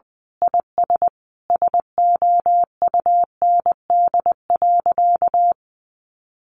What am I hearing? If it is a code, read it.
Morse code, heard as a keyed tone, about 20 wpm, 700 Hz. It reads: IS SOUND.
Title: Sentences from Top 1400 Words 20wpm